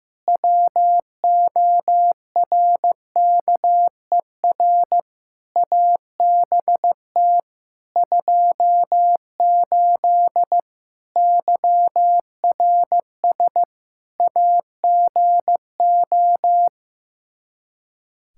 Morse code, heard as a keyed tone, about 15 wpm, 700 Hz. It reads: WORKER ABT 28 YRS AGO
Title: K1USN SST QSO Practice 15wpm